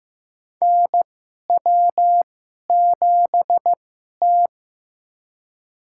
Morse code, heard as a keyed tone, about 15 wpm, spaced as for 11 wpm, 700 Hz. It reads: NW7T
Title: USA Call Signs - 4 Characters - 2x Character Spacing - 15wpm